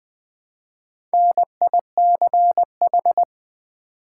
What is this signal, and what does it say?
Morse code, heard as a keyed tone, about 20 wpm, 700 Hz. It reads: NICH
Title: Build A Word - 1 Letter At A Time - 5-Letter Words - 20wpm